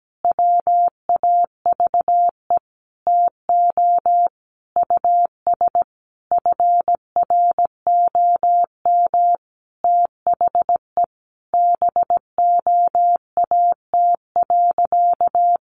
Morse code, heard as a keyed tone, about 17 wpm, 700 Hz. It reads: WAVE TO US FROM THE BOAT.
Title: Sentences from Top 1400 Words 17wpm